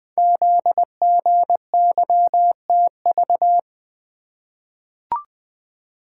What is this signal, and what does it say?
Morse code, heard as a keyed tone, about 20 wpm, 700 Hz. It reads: ZGYTV
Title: Code Groups - 5 Letters - 20wpm